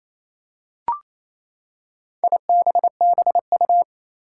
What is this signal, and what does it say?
Morse code, heard as a keyed tone, about 28 wpm, 700 Hz. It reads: IBBU
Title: Four Challenging Letter-Number Combinations 28wpm